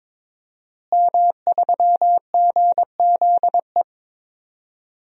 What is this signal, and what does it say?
Morse code, heard as a keyed tone, about 22 wpm, 700 Hz. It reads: M3GZE
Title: DXCC-100 Call Signs - 5 Characters - Repeated 2x - 22wpm